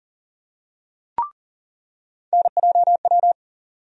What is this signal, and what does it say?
Morse code, heard as a keyed tone, about 40 wpm, 700 Hz. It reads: NJW
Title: USA Call Sign Postfixes - 3 Letters 40wpm